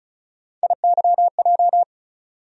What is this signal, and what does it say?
Morse code, heard as a keyed tone, about 34 wpm, 700 Hz. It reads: IYJ
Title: USA Call Sign Postfixes - 3 Letters 35wpm